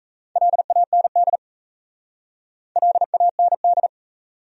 Morse code, heard as a keyed tone, about 36 wpm, 700 Hz. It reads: LAND LAND